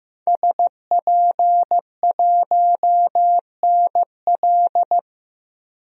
Morse code, heard as a keyed tone, about 15 wpm, 700 Hz. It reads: SP1NL
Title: DXCC-100 Call Signs - 5 Characters - 15wpm